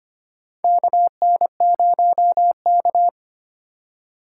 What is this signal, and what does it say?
Morse code, heard as a keyed tone, about 25 wpm, 700 Hz. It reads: KN0K
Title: USA Call Signs - 4 Characters - 4x Character Spacing - 25wpm